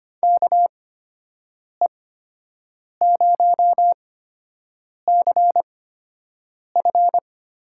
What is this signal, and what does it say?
Morse code, heard as a keyed tone, about 25 wpm, spaced as for 7 wpm, 700 Hz. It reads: KE0CF
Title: USA Call Signs - 6 Characters - 8x Character Spacing - 25wpm